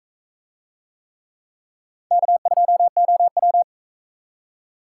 Morse code, heard as a keyed tone, about 43 wpm, 700 Hz. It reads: K2OW